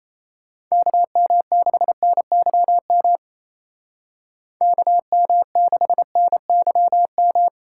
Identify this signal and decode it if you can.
Morse code, heard as a keyed tone, about 36 wpm, 700 Hz. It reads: KM6NYM KM6NYM